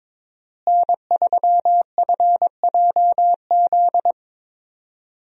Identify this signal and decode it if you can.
Morse code, heard as a keyed tone, about 22 wpm, 700 Hz. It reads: N3FJZ
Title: USA Call Signs - 5 Characters - Repeated 2x - 22wpm